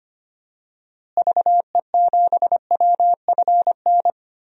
Morse code, heard as a keyed tone, about 25 wpm, 700 Hz. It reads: VE7WFN
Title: DXCC-100 Call Signs - 6 Characters - Repeated 2x - 25wpm